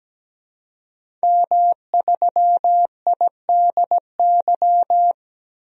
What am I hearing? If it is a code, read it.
Morse code, heard as a keyed tone, about 17 wpm, 700 Hz. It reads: M3IDY
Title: DXCC-100 Call Signs - 5 Characters - 17wpm